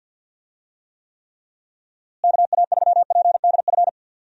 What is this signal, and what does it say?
Morse code, heard as a keyed tone, about 52 wpm, 700 Hz. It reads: KA3PDF
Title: USA Call Signs - 6 Characters - Repeated 3x - 50wpm